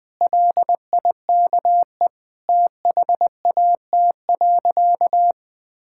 Morse code, heard as a keyed tone, about 20 wpm, 700 Hz. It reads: LIKE THAT.
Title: Sentences with 4-Word Phrases - Spoken Beginning - 20wpm